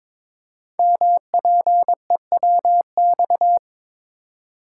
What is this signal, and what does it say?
Morse code, heard as a keyed tone, about 22 wpm, 700 Hz. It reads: MPEWX
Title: Code Groups - 5 Letters - 22wpm